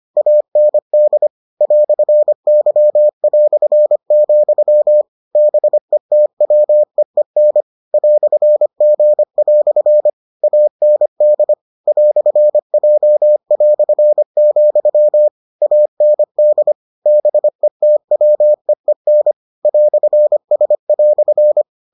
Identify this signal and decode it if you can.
Morse code, heard as a keyed tone, about 25 wpm, 600 Hz. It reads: AND "Y", BETWEEN "G" AND "J", AND BETWEEN "S"